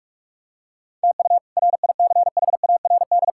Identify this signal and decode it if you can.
Morse code, heard as a keyed tone, about 42 wpm, 700 Hz. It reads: TU RICHARD
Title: K1USN SST QSO Practice - 45wpm